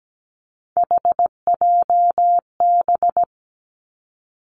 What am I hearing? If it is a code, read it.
Morse code, heard as a keyed tone, about 17 wpm, 700 Hz. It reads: HJB
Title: Three Challenging Letter-Number Combinations 17wpm